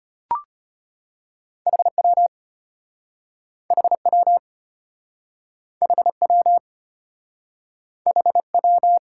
Morse code, heard as a keyed tone, about 34 wpm, 700 Hz. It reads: HW HW HW HW